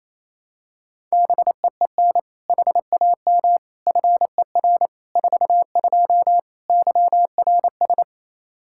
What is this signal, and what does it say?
Morse code, heard as a keyed tone, about 28 wpm, 700 Hz. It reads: BEEN HAM FER 42 YRS
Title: Practice QSO - Exchange 3 - More Info - 28wpm